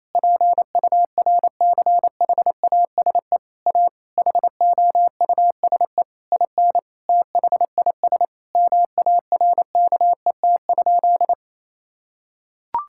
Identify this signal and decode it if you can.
Morse code, heard as a keyed tone, about 28 wpm, 700 Hz. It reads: PURCHASE A HOUSE IN THIS MARKET?
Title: Sentences from Top 1000 Words - Review Part 5 - 28wpm